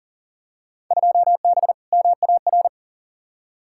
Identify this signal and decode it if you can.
Morse code, heard as a keyed tone, about 40 wpm, 700 Hz. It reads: 2B MAR
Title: Field Day Exchanges Encore - 40wpm